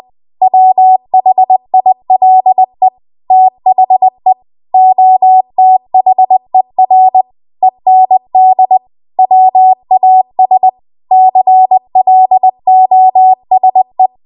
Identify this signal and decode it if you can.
Morse code, heard as a keyed tone, about 20 wpm, 750 Hz. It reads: WHILE THE OTHER END WAS CLOSE
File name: Archive / 20 WPM / 20WPM